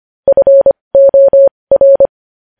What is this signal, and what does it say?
Morse code, heard as a keyed tone, about 25 wpm, spaced as for 20 wpm, 550 Hz. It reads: FOR
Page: News Headlines in Morse Code at 20 WPM